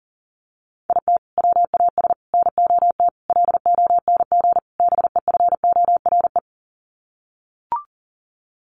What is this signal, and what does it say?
Morse code, heard as a keyed tone, about 40 wpm, 700 Hz. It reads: IT WAS NOT LONG BEFORE
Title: Sets of 5 Words Encore2 40wpm